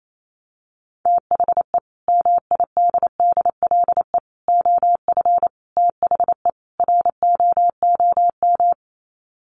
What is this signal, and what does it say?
Morse code, heard as a keyed tone, about 28 wpm, 700 Hz. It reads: THE MIDDLE OF THE ROOM
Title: Sets of 5 Words Encore2 28wpm